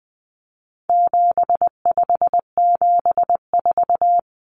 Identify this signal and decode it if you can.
Morse code, heard as a keyed tone, about 20 wpm, 700 Hz. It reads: 7574
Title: Common Cut Numbers - 4-Digit 20wpm